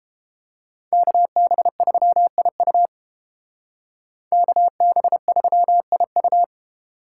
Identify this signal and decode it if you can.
Morse code, heard as a keyed tone, about 30 wpm, 700 Hz. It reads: KB3IU KB3IU